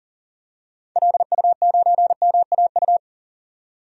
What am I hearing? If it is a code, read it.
Morse code, heard as a keyed tone, about 37 wpm, 700 Hz. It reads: LU9MAU
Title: DXCC-100 Call Signs - 6 Characters - 40wpm